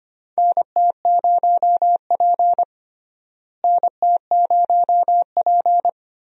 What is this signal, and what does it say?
Morse code, heard as a keyed tone, about 25 wpm, 700 Hz. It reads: NT0P NT0P
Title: USA Call Signs - 4 Characters - Repeated 3x - 25wpm